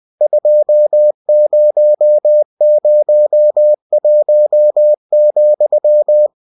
Morse code, heard as a keyed tone, about 20 wpm, 600 Hz. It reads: 2001,
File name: cw-books / winnie-the-pooh / winnie-20wpm-page-0120